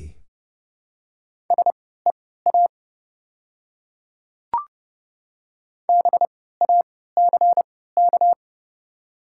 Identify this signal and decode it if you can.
Morse code, heard as a keyed tone, about 30 wpm, spaced as for 17 wpm, 700 Hz. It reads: SEA BACK
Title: 3x Character Spacing - Top 300 Words - 30wpm